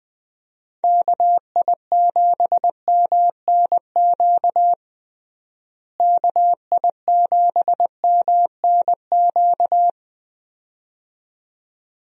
Morse code, heard as a keyed tone, about 21 wpm, 700 Hz. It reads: KI7MNQ KI7MNQ
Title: USA Call Signs - 6 Characters - Repeated 2x - 20wpm